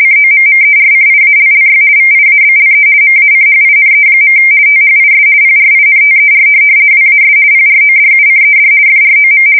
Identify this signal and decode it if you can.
RTTY, about 50 baud. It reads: COMMERCIALLY VIABLE MACHINE. THIS PRINTER WAS PRODUCED FROM
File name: fsk170-50B / teletype-corp